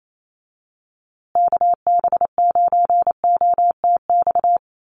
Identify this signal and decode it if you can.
Morse code, heard as a keyed tone, about 28 wpm, 700 Hz. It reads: KB9OTX
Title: USA Call Signs Repeated 2x 28wpm